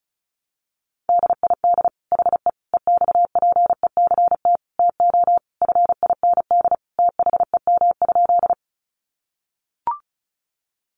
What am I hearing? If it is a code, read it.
Morse code, heard as a keyed tone, about 34 wpm, 700 Hz. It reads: DID HE EXPECT TO FIND THEM?
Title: Sentences from Top 1100 Words 35wpm